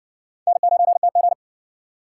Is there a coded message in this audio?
Yes: Morse code from 0.5 to 1.3 s, heard as a keyed tone, about 60 wpm, 700 Hz. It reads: N9TG